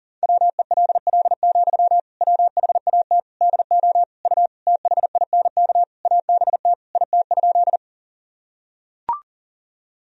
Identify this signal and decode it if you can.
Morse code, heard as a keyed tone, about 40 wpm, 700 Hz. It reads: WELL, WHAT DO U THINK ABT IT?
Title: Sentences from Top 200 Words and Abbreviations 40wpm